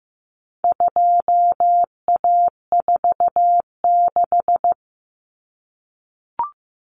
Morse code, heard as a keyed tone, about 15 wpm, 700 Hz. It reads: 2A46
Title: Common Cut Numbers - 4-Digit 15wpm